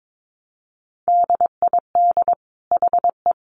Morse code, heard as a keyed tone, about 22 wpm, 700 Hz. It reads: DID HE
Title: Sentences from Top 400 Words 22wpm